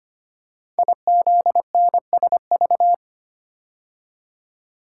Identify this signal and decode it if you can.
Morse code, heard as a keyed tone, about 25 wpm, 700 Hz. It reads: IZNSV
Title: Code Groups - 5 Letters - 25wpm